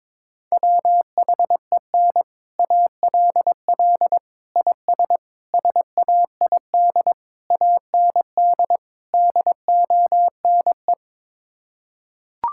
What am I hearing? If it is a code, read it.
Morse code, heard as a keyed tone, about 22 wpm, 700 Hz. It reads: WHEN ALL IS SAID AND DONE
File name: practice / 22wpm / Sets of 6 Words - 22wpm